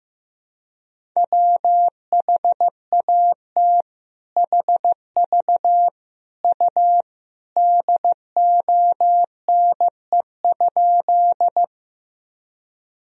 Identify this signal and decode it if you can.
Morse code, heard as a keyed tone, about 16 wpm, 700 Hz. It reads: WHAT HV U DONE?
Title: Sentences from Top 500 Words and Abbreviations 15wpm